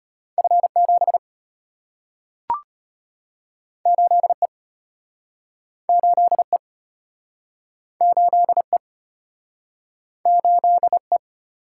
Morse code, heard as a keyed tone, about 34 wpm, 700 Hz. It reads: F7 8E 8E 8E 8E